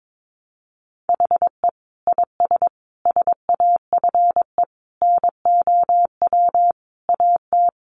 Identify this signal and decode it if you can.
Morse code, heard as a keyed tone, about 22 wpm, 700 Hz. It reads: HE IS SAFE NOW AT
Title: Sentences from Top 900 Words 22wpm